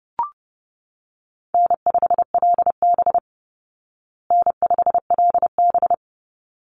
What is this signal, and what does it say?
Morse code, heard as a keyed tone, about 30 wpm, 700 Hz. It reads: N5LB N5LB
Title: Easy Call Signs Repeated 2x - 30wpm